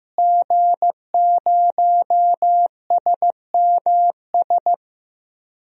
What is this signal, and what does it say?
Morse code, heard as a keyed tone, about 15 wpm, 700 Hz. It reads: G0SMS
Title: DXCC-100 Call Signs - 5 Characters - Repeated 2x - 15wpm